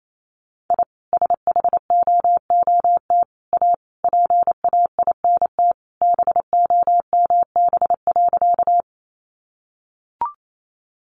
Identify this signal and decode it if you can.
Morse code, heard as a keyed tone, about 28 wpm, 700 Hz. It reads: I SHOOT A PAINT BOMB.
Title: A Sound Of Thunder 28wpm